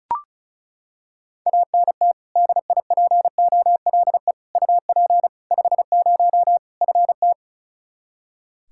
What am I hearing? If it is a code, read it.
Morse code, heard as a keyed tone, about 34 wpm, 700 Hz. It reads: ANT DIPOLE UP 50 FT
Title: Practice QSO - Exchange 2 - Info - 35wpm